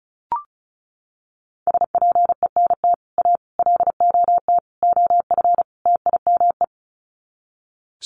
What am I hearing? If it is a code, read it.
Morse code, heard as a keyed tone, about 34 wpm, 700 Hz. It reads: SPENT A LOT OF TIME
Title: Sets of 5 Words 35wpm